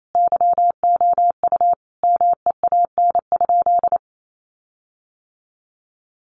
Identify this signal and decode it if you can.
Morse code, heard as a keyed tone, about 28 wpm, 700 Hz. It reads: YOU MEAN?
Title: Sentences from Top 1300 Words 28wpm